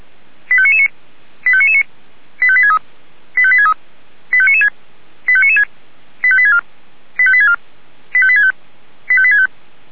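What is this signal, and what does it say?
Five-tone selective call (ZVEI), heard as quick runs of five tones, tones of about 70 ms. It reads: 85999 85999 85863 85863 85996 85996 85864 85864 85865 85865
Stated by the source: ZVEI five-tone selective call received on an SDR, archived by Feuerwehr Orlamünde (fire brigade).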